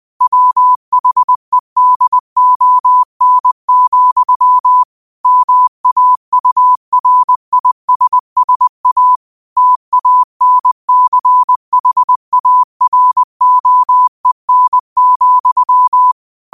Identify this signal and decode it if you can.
Morse code, heard as a keyed tone, about 20 wpm, 1000 Hz. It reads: WHEDON, MAURISSA TANCHAROEN,
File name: qotdfiles / 20wpm